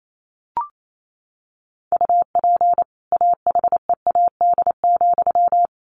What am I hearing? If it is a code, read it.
Morse code, heard as a keyed tone, about 28 wpm, 700 Hz. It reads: UP AHEAD,
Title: A Sound Of Thunder 28wpm